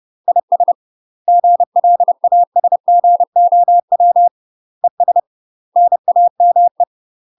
Morse code, heard as a keyed tone, about 30 wpm, 700 Hz. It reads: IS GLASGOW ES NAME
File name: UK MASTER QSOs TRAINING / 30WPM / UK MASTER QSOs 0012